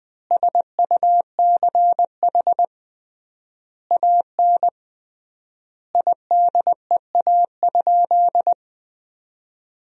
Morse code, heard as a keyed tone, about 20 wpm, 700 Hz. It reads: SUCH AN IDEA?
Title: 3x Word-Spacing - Sentences from Top 400 Words - 20wpm